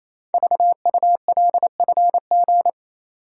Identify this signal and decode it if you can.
Morse code, heard as a keyed tone, about 28 wpm, 690 Hz. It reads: VULFG
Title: Code Groups - 5 Letters - 28wpm